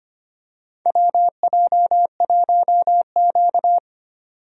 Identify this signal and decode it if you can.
Morse code, heard as a keyed tone, about 25 wpm, 700 Hz. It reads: WJ1Q
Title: USA Call Signs - 4 Characters - 25wpm